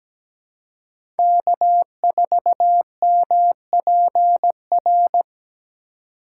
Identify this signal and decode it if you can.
Morse code, heard as a keyed tone, about 17 wpm, 700 Hz. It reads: K4MPR
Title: USA Call Signs - 6x Character Spacing - 17wpm